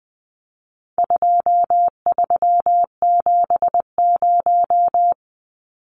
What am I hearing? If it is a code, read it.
Morse code, heard as a keyed tone, about 20 wpm, 700 Hz. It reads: 2370
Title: Numbers - 4-Digit 20wpm